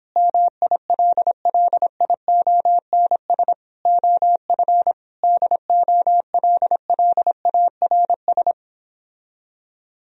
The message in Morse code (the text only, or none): MILLIONS OF DOLLARS